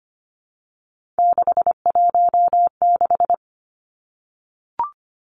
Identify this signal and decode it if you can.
Morse code, heard as a keyed tone, about 25 wpm, 700 Hz. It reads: 616
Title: Three Challenging Letter-Number Combinations 25wpm